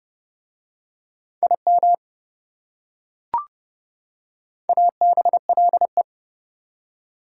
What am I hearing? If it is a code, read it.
Morse code, heard as a keyed tone, about 29 wpm, 700 Hz. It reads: IM ABLE